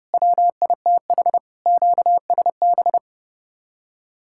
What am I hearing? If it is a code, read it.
Morse code, heard as a keyed tone, about 35 wpm, 700 Hz. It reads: WITH QSB
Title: Speed-Racing Common QSO Phrases - 30wpm